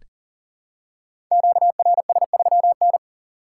Morse code, heard as a keyed tone, about 38 wpm, 700 Hz. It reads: QRS3N
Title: Code Groups - 5 Letter:Numbers - 40wpm